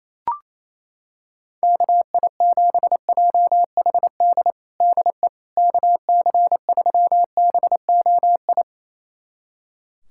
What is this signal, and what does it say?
Morse code, heard as a keyed tone, about 28 wpm, 700 Hz. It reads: KI7JHD DE KC3BOI
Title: USA Callsign DE Callsign 28wpm